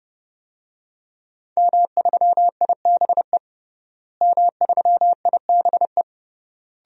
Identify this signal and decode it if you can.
Morse code, heard as a keyed tone, about 30 wpm, 700 Hz. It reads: M3IBE M3IBE
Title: DXCC-100 Call Signs - 5 Characters - Repeated 3x - 30wpm